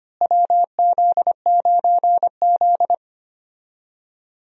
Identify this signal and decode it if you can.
Morse code, heard as a keyed tone, about 25 wpm, 700 Hz. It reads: WZ9Z
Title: USA Call Signs - 4 Characters - Repeated 3x - 25wpm